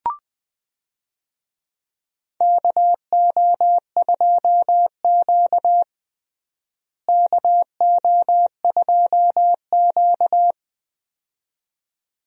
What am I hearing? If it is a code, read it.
Morse code, heard as a keyed tone, about 20 wpm, 700 Hz. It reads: KO2Q KO2Q
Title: USA Call Signs - 4 Characters - Repeated 2x - 20wpm